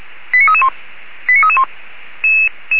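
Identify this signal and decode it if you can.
Five-tone selective call (ZVEI), heard as quick runs of five tones, tones of about 70 ms. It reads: 89331 89331
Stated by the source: ZVEI five-tone selective call 89300 received by SDR.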